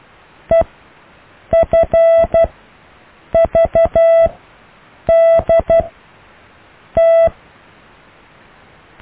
Morse code, heard as a keyed tone, about 12 wpm, spaced as for 7 wpm, 660 Hz. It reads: EFVDT
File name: cw / EKTPV+BFHNW